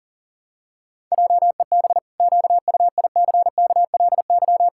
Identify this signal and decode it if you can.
Morse code, heard as a keyed tone, about 40 wpm, 700 Hz. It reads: JEB QUICKLY
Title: Pangram Challenge v2 - 40wpm